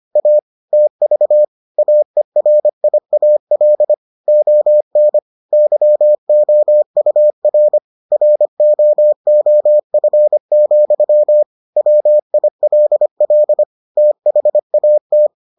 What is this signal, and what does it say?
Morse code, heard as a keyed tone, about 25 wpm, 600 Hz. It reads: A TV AERIAL ON YOUR ROOF, WILL THAT